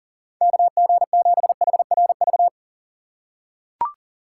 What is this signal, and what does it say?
Morse code, heard as a keyed tone, about 40 wpm, 700 Hz. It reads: KG7HRV
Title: USA Call Signs - 6 Characters - 2x Character Spacing - 40wpm